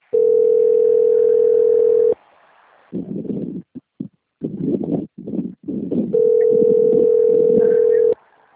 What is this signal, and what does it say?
Ringback tone (North America), 440 + 480 Hz.